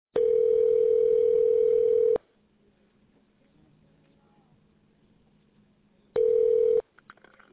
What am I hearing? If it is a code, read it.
Ringback tone (North America), 440 + 480 Hz.